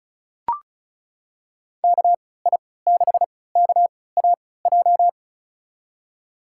Morse code, heard as a keyed tone, about 36 wpm, spaced as for 20 wpm, 700 Hz. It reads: KI6KAJ
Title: USA Call Signs - 3x Character Spacing - 35wpm